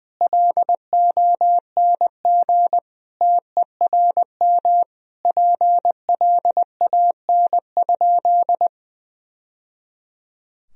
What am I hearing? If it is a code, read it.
Morse code, heard as a keyed tone, about 20 wpm, 700 Hz. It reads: LONG TERM PLAN?
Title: Sentences from Top 1000 Words - Review Part 4 - 20wpm